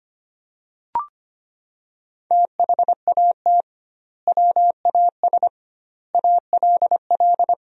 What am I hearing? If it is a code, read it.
Morse code, heard as a keyed tone, about 25 wpm, 700 Hz. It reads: THAT WAS ALL
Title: 2x Word-Spacing - Sentences from Top 100 Words - 25wpm